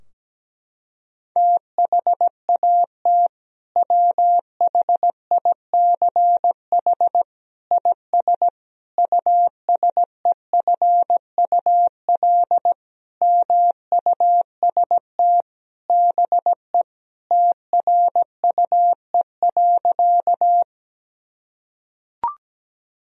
Morse code, heard as a keyed tone, about 17 wpm, 700 Hz. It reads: THAT WHICH IS USEFUL MUST BE TRUE.
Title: Sentences from Top 1000 Words - Review Part 4 - 17wpm